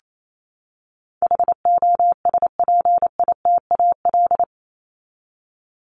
Morse code, heard as a keyed tone, about 28 wpm, 700 Hz. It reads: HOSPITAL
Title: Top 1751-2000 Words 28wpm